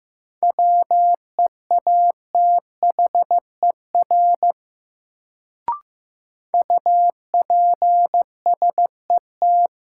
Morse code, heard as a keyed tone, about 15 wpm, 700 Hz. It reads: WEATHER UPSET
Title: Sentences from Top 800 Words - New Word Drill - 15wpm